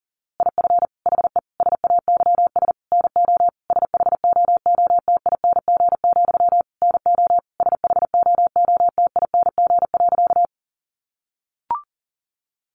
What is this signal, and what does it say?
Morse code, heard as a keyed tone, about 40 wpm, 700 Hz. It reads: IF HE SAYS NO SHOOTING, NO SHOOTING.
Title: A Sound Of Thunder 40wpm